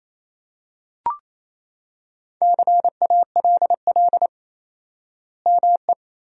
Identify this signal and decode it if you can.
Morse code, heard as a keyed tone, about 28 wpm, 700 Hz. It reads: CALL ME
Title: 4x Word-Spacing - Sentences from Top 200 Words - 28wpm